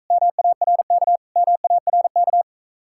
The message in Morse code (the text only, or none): MARK MARK